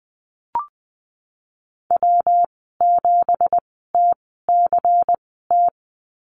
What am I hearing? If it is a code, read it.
Morse code, heard as a keyed tone, about 20 wpm, spaced as for 15 wpm, 700 Hz. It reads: W7TCT
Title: 193 Lesson 31g - Letter v - Call Signs 20wpm